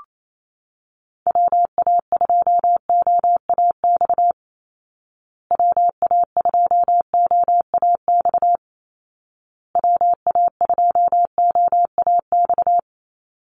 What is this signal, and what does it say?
Morse code, heard as a keyed tone, about 28 wpm, 700 Hz. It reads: WA2OAX WA2OAX WA2OAX
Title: Call Signs Repeated 2x - 28wpm